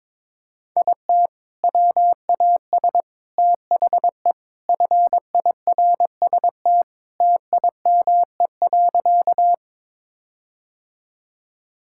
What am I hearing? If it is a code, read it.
Morse code, heard as a keyed tone, about 22 wpm, 700 Hz. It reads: IT WAS THE FIRST TIME.
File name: practice / 22wpm / Sentences with 4-Word Phrases - 22wpm